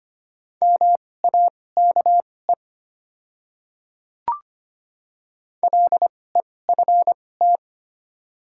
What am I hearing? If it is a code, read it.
Morse code, heard as a keyed tone, about 25 wpm, spaced as for 18 wpm, 700 Hz. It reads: MAKE LEFT
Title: 2x Character Spacing - Top 300 Words - 25wpm